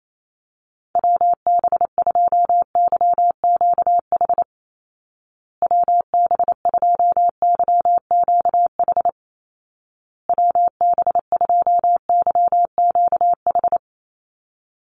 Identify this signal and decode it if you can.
Morse code, heard as a keyed tone, about 28 wpm, 700 Hz. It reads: WB2YQH WB2YQH WB2YQH
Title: Call Signs Repeated 2x - 28wpm